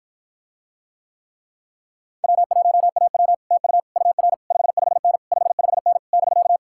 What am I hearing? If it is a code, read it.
Morse code, heard as a keyed tone, about 54 wpm, 700 Hz. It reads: W1AW TU UR 55N 55N <BK>